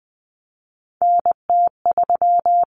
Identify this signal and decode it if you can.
Morse code, heard as a keyed tone, about 20 wpm, 700 Hz. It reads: NT3
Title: Common Cut Numbers - 3-Digit 20wpm